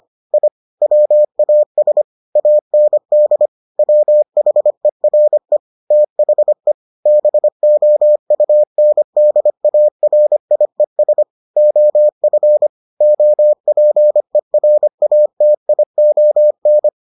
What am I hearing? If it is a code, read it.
Morse code, heard as a keyed tone, about 25 wpm, 600 Hz. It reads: I WAS AND WHERE THE BOUNDARIES OF OPERATION